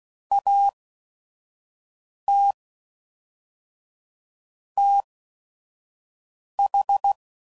Morse code, heard as a keyed tone, about 16 wpm, spaced as for 5 wpm, 800 Hz. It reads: AT TH
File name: CW Code Groups @5WPM / Technical Subjects / AND Gate